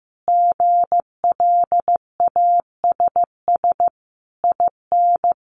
Morse code, heard as a keyed tone, about 15 wpm, 700 Hz. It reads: GLASS IN
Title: Sentences from Top 1300 Words 15wpm